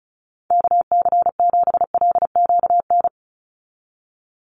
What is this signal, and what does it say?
Morse code, heard as a keyed tone, about 36 wpm, 700 Hz. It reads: KC7LQN
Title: USA Call Signs 35wpm